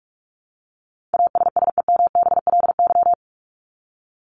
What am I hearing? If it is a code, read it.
Morse code, heard as a keyed tone, about 44 wpm, 700 Hz. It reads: ASSEMBLY
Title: LY Suffix - Top 100 Words with Base Words 45wpm